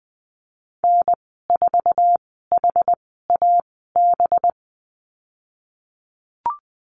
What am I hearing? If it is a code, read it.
Morse code, heard as a keyed tone, about 20 wpm, spaced as for 15 wpm, 700 Hz. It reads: N4HAB
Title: 186 Lesson 30g - Letter b - Call Signs 20wpm